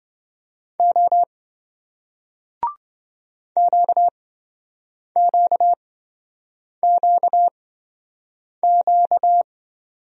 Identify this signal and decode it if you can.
Morse code, heard as a keyed tone, about 28 wpm, spaced as for 7 wpm, 700 Hz. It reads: O QQQQ